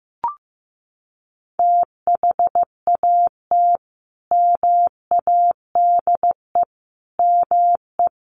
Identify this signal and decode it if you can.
Morse code, heard as a keyed tone, about 15 wpm, 700 Hz. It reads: THAT MADE ME
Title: Sentences from Top 1000 Words 15wpm